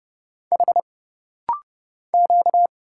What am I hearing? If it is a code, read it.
Morse code, heard as a keyed tone, about 30 wpm, 700 Hz. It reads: H Q
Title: Single Letters - Rapid-Fire 30wpm